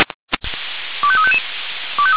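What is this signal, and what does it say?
Five-tone selective call (ZVEI), heard as quick runs of five tones, tones of about 70 ms. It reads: 25355
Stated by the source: ZVEI five-tone selective call 25299, received by SDR.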